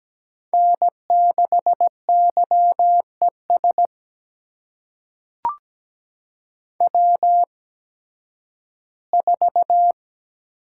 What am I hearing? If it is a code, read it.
Morse code, heard as a keyed tone, about 17 wpm, spaced as for 11 wpm, 700 Hz. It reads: N6YES W4
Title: USA Call Signs - 5 Characters - 8x Character Spacing - 17wpm